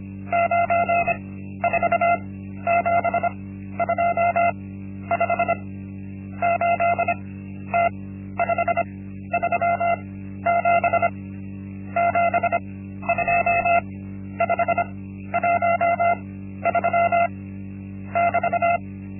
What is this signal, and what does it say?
Morse code, heard as a keyed tone, about 25 wpm, spaced as for 12 wpm, 690 Hz. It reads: 94725 8T537 72513 <BT>
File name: Linksmakalnis-1